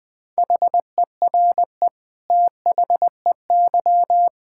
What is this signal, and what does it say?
Morse code, heard as a keyed tone, about 19 wpm, 700 Hz. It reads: HERE THEY
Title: Sentences from Top 1600 Words 20wpm